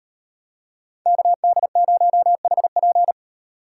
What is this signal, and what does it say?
Morse code, heard as a keyed tone, about 38 wpm, 700 Hz. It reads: KD0HP